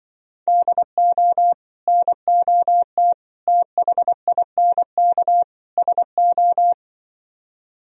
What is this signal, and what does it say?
Morse code, heard as a keyed tone, about 24 wpm, 700 Hz. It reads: DO NOT THINK SO
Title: Speed-Racing - Sets of 4 Words 20wpm